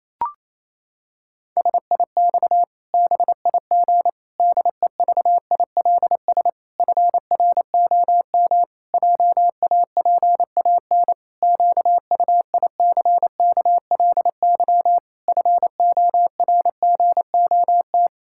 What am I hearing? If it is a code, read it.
Morse code, heard as a keyed tone, about 28 wpm, 700 Hz. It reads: SIX BIG DEVILS FROM JAPAN QUICKLY FORGOT
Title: Pangram Challenge v2 - 28wpm